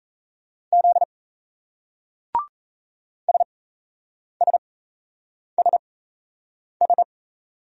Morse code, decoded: Z SSSS

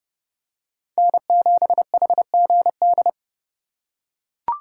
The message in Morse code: N7HGD

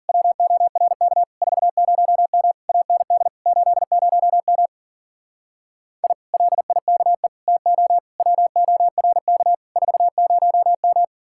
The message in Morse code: WORK 40M AND 80M I LIKE TO WORK 40M

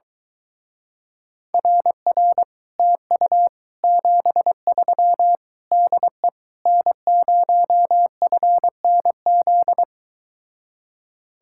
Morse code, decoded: RR TU 73 DE N0FNZ

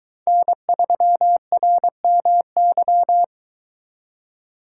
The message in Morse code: N3RMY

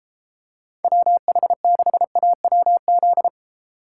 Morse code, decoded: WH6AWZ

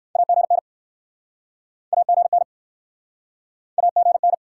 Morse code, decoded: AGN AGN AGN

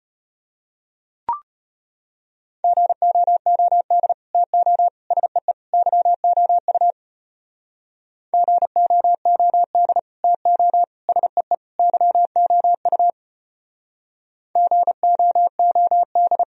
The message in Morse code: GOOD TO SEE YOU GOOD TO SEE YOU GOOD

